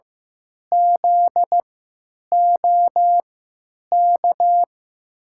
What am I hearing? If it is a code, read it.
Morse code, heard as a keyed tone, about 15 wpm, spaced as for 9 wpm, 700 Hz. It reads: ZOK
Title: USA Call Signs - 6 Characters - 3x Character Spacing - 15wpm